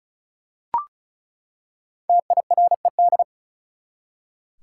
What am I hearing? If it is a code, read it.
Morse code, heard as a keyed tone, about 36 wpm, 700 Hz. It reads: TIRED
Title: Five Letter Words - 35wpm